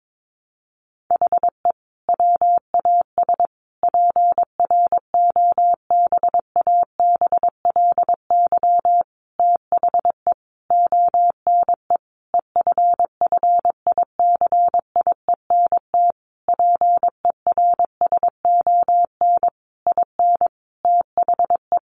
Morse code, heard as a keyed tone, about 22 wpm, 700 Hz. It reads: HE WAS PROBABLY THE ONE EFFICIENT PERSON IN THE
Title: Sentences from Top 1300 Words 22wpm